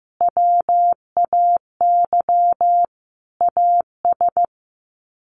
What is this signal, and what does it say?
Morse code, heard as a keyed tone, about 15 wpm, 700 Hz. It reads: WAY AS
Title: Sets of 5 Words Encore 15wpm